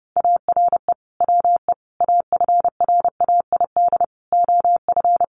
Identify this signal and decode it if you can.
Morse code, heard as a keyed tone, about 30 wpm, 700 Hz. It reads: ARE WE AFRAID OF
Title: Sentences from Top 1500 Words 30wpm